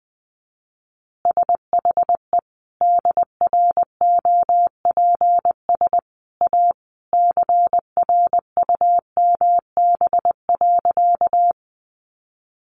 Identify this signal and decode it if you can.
Morse code, heard as a keyed tone, about 20 wpm, 700 Hz. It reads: SHE DROPS A CRUMB.
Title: Additional Easy Phrases 20wpm